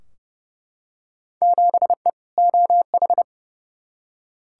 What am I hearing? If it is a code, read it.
Morse code, heard as a keyed tone, about 30 wpm, 700 Hz. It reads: 7E OH
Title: Field Day Exchanges Encore - 30wpm